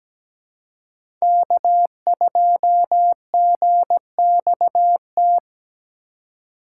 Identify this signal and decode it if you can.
Morse code, heard as a keyed tone, about 17 wpm, 700 Hz. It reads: K2GXT
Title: USA Call Signs - 5 Characters - 2x Character Spacing - 17wpm